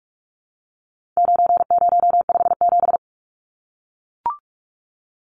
Morse code, heard as a keyed tone, about 44 wpm, 700 Hz. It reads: 9057
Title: Numbers - 4-Digit 45wpm